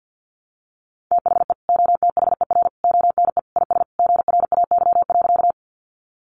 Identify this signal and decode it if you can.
Morse code, heard as a keyed tone, about 50 wpm, 700 Hz. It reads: THE OTHER ONE IS GRAY.
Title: Additional Easy Phrases 50wpm